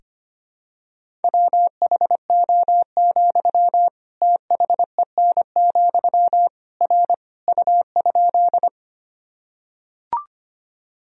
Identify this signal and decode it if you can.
Morse code, heard as a keyed tone, about 25 wpm, 700 Hz. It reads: WHO, THEN, R U?
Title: Sentences from Top 100 Words and Abbreviations 25wpm